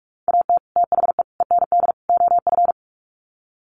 Morse code, heard as a keyed tone, about 44 wpm, 700 Hz. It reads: AT THE END OF
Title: Sets of 4 Words 45wpm